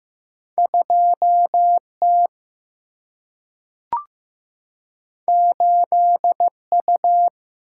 Morse code, heard as a keyed tone, about 15 wpm, 700 Hz. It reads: 2T 8U
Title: Single-Letter Single-Number 15wpm